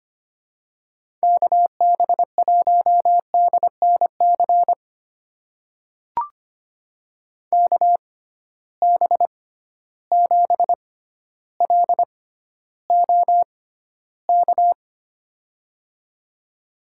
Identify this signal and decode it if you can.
Morse code, heard as a keyed tone, about 25 wpm, spaced as for 13 wpm, 700 Hz. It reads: KB1DNC KB7LOK
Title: USA Call Signs - 6 Characters - 6x Character Spacing - 25wpm